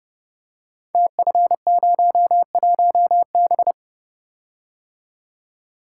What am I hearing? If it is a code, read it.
Morse code, heard as a keyed tone, about 30 wpm, 700 Hz. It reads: TF01B
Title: Code Groups - 5 Letter:Numbers - 30wpm